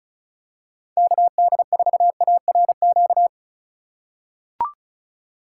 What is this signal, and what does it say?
Morse code, heard as a keyed tone, about 36 wpm, 700 Hz. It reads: KD4ARQ
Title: USA Call Signs - 6 Characters - 4x Character Spacing - 35wpm